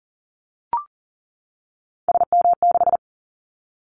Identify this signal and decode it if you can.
Morse code, heard as a keyed tone, about 40 wpm, 700 Hz. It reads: SM6
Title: Callsign Prefixes for Top 25 DXCC - 40wpm